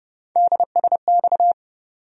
Morse code, heard as a keyed tone, about 30 wpm, 700 Hz. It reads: DSX
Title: USA Call Sign Postfixes - 3 Letters 30wpm